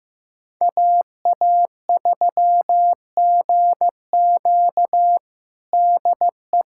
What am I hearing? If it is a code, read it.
Morse code, heard as a keyed tone, about 15 wpm, 700 Hz. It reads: AA3GQ DE
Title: USA Callsign DE Callsign - 5 Characters - 15wpm